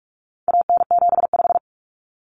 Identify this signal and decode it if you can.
Morse code, heard as a keyed tone, about 44 wpm, 700 Hz. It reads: AN75
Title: Common Cut Numbers - 4-Digit 45wpm